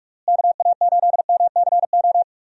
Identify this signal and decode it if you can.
Morse code, heard as a keyed tone, about 46 wpm, 700 Hz. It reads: KA8MCO
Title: USA Call Signs - 6 Characters - Repeated 3x - 45wpm